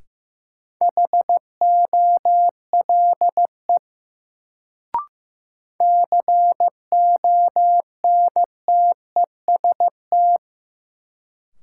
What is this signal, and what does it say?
Morse code, heard as a keyed tone, about 16 wpm, 700 Hz. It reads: HOLE CONTEST
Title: Sentences from Top 1600 Words - New Word Drill Alt - 15wpm